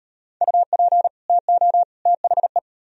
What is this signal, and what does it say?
Morse code, heard as a keyed tone, about 37 wpm, 700 Hz. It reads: UP TO THE